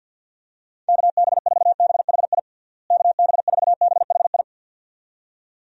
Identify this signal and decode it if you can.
Morse code, heard as a keyed tone, about 50 wpm, 700 Hz. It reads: KB4BHI KB4BHI
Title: USA Call Signs - 6 Characters - Repeated 2x - 50wpm